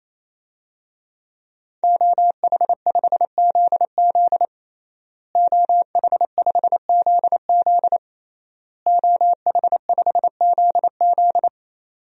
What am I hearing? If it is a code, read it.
Morse code, heard as a keyed tone, about 28 wpm, 700 Hz. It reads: OH5ZZ OH5ZZ OH5ZZ
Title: DXCC-100 Call Signs - 5 Characters - Repeated 3x - 28wpm